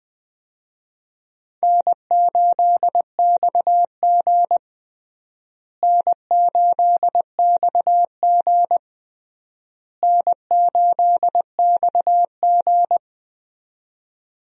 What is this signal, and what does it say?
Morse code, heard as a keyed tone, about 20 wpm, 700 Hz. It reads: N8XG N8XG N8XG
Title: USA Call Signs - 4 Characters - Repeated 3x - 20wpm